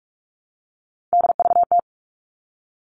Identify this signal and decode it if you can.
Morse code, heard as a keyed tone, about 44 wpm, 700 Hz. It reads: DVT
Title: All Cut Numbers - 3-Digit 45wpm